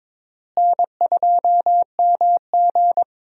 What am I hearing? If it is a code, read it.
Morse code, heard as a keyed tone, about 22 wpm, 700 Hz. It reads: N2MG
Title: USA Call Signs - 4 Characters - Repeated 3x - 22wpm